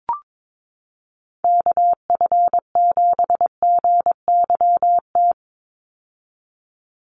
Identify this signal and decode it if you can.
Morse code, heard as a keyed tone, about 22 wpm, 700 Hz. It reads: KF7GYT
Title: USA Call Signs 22wpm